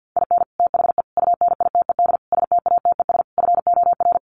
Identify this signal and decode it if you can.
Morse code, heard as a keyed tone, about 50 wpm, 700 Hz. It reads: IN THE UNITED STATES FOR
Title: Sets of 5 Words Encore2 50wpm